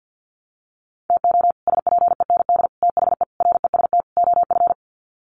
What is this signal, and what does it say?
Morse code, heard as a keyed tone, about 50 wpm, 700 Hz. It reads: TO SPEND THE REST OF
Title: Sets of 5 Words Encore2 50wpm